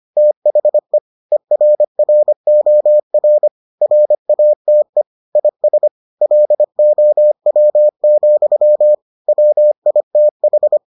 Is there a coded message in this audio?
Yes: Morse code from 0.2 to 10.8 s, heard as a keyed tone, about 25 wpm, 600 Hz. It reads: THE ERROR RATE IS LOW, WITH